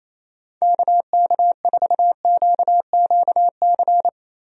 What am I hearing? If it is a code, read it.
Morse code, heard as a keyed tone, about 28 wpm, 700 Hz. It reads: KK4QQC
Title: USA Call Signs - 6 Characters - Repeated 3x - 28wpm